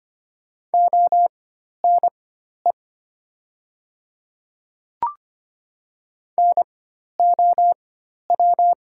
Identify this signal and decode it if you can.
Morse code, heard as a keyed tone, about 25 wpm, spaced as for 12 wpm, 700 Hz. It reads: ONE NOW
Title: 4x Character Spacing - Top 100 Words - 25wpm